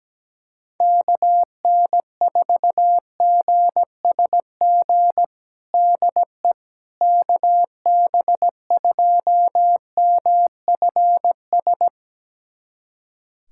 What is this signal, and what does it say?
Morse code, heard as a keyed tone, about 17 wpm, 700 Hz. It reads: KN4GSG DE KB2MFS
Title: USA Callsign DE Callsign - 6 Characters - 17wpm